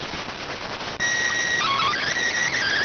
SSTV (slow-scan television), PD 290.